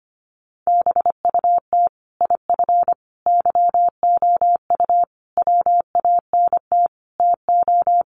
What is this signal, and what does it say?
Morse code, heard as a keyed tone, about 25 wpm, 700 Hz. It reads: BUT IF YOU WANT TO
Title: Sentences from Top 1200 Words 25wpm